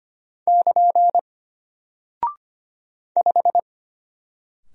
Morse code, heard as a keyed tone, about 25 wpm, 700 Hz. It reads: <KN> 5
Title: Single Character (Letter:Number:Punctuation:Prosign) Alt 25wpm